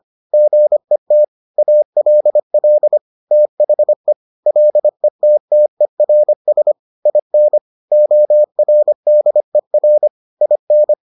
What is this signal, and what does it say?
Morse code, heard as a keyed tone, about 25 wpm, 600 Hz. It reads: GET ALL THE LETTERS IN ORDER IN